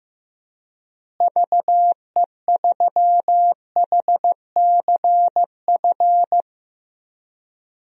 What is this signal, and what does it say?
Morse code, heard as a keyed tone, about 15 wpm, 700 Hz. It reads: VE3HCF
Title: DXCC-100 Call Signs - 6 Characters - Repeated 3x - 15wpm